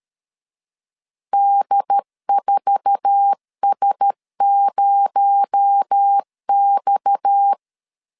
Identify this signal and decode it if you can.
Morse code, heard as a keyed tone, about 13 wpm, 800 Hz. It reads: D4S0X
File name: LEZIONI / 8 YJ45 / 8 YJ4 12wpm 2151